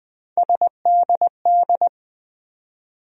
Morse code, heard as a keyed tone, about 20 wpm, 700 Hz. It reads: SDD
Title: USA Call Sign Postfixes - 3 Letters 20wpm